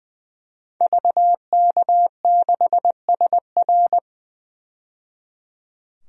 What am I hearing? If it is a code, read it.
Morse code, heard as a keyed tone, about 20 wpm, 700 Hz. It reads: VK6SR
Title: DXCC-100 Call Signs - 5 Characters - 20wpm